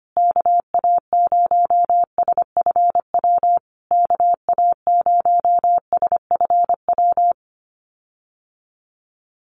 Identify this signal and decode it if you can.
Morse code, heard as a keyed tone, about 25 wpm, 700 Hz. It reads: KA0SFW KA0SFW
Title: USA Call Signs Repeated 3x 25wpm